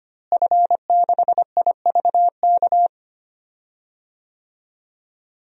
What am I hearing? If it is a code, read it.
Morse code, heard as a keyed tone, about 25 wpm, 700 Hz. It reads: F6IVK
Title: DXCC-100 Call Signs - 5 Characters - Repeated 3x - 25wpm